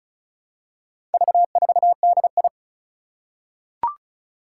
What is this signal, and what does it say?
Morse code, heard as a keyed tone, about 36 wpm, 700 Hz. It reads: V4DI
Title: Four Challenging Letter-Number Combinations 35wpm